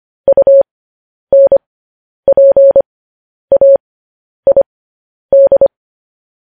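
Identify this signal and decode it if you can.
Morse code, heard as a keyed tone, about 25 wpm, spaced as for 10 wpm, 550 Hz. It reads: UNPAID
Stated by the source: Morse code transmission of news headlines.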